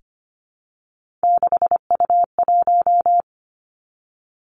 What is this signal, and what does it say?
Morse code, heard as a keyed tone, about 25 wpm, 700 Hz. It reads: 6U1
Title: Three Challenging Letter-Number Combinations 25wpm